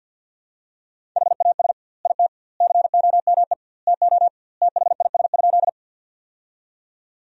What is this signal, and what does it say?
Morse code, heard as a keyed tone, about 47 wpm, 700 Hz. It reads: HAS IT COME TO THIS?
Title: 2x Word-Spacing - Sentences from Top 100 Words - 50wpm